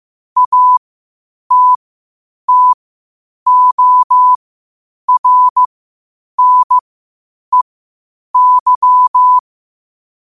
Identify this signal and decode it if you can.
Morse code, heard as a keyed tone, about 15 wpm, spaced as for 8 wpm, 1000 Hz. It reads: ATTORNEY
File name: qotdfiles / 8wpm